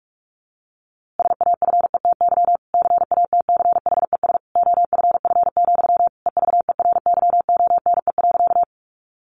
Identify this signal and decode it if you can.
Morse code, heard as a keyed tone, about 44 wpm, 700 Hz. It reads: SAFETY CATCHES OFF, EVERYONE.
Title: A Sound Of Thunder 45wpm